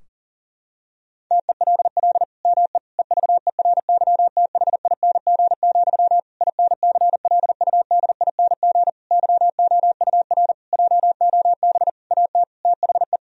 Morse code, heard as a keyed tone, about 40 wpm, 700 Hz. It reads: TELL ME EVERYTHING, INCLUDING YOUR JOB AT THE
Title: Sentences from Top 1000 Words - Review Part 4 - 40wpm